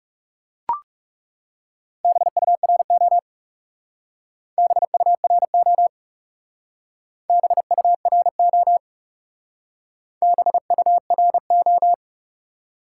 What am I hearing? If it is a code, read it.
Morse code, heard as a keyed tone, about 36 wpm, 700 Hz. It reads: BURO BURO BURO BURO
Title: Speed-Racing Top 90 QSO Elements 30wpm